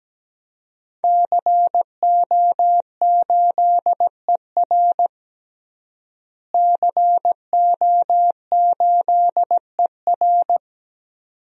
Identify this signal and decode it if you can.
Morse code, heard as a keyed tone, about 17 wpm, 700 Hz. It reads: CO8ER CO8ER
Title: DXCC-100 Call Signs - 5 Characters - Repeated 2x - 17wpm